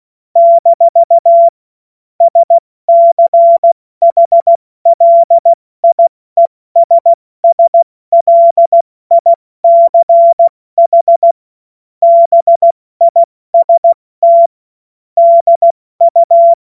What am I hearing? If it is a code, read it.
Morse code, heard as a keyed tone, about 16 wpm, 680 Hz. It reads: - SCHLIESSLICH BIST DU
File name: Morsekurs / Witze / Witz 16 April 2024 70 F80